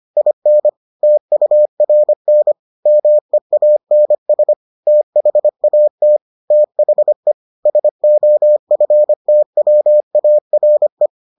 Morse code, heard as a keyed tone, about 25 wpm, 600 Hz. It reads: IN TURN MEANS THAT THE SOFTWARE